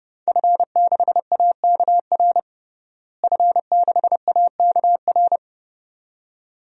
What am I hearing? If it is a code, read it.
Morse code, heard as a keyed tone, about 31 wpm, 700 Hz. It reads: F6AKR F6AKR
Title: DXCC-100 Call Signs - 5 Characters - Repeated 2x - 30wpm